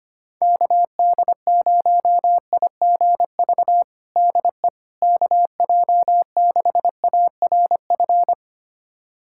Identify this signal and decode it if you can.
Morse code, heard as a keyed tone, about 25 wpm, 700 Hz. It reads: KD0IGV DE KJ6ARF
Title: USA Callsign DE Callsign - 6 Characters - 25wpm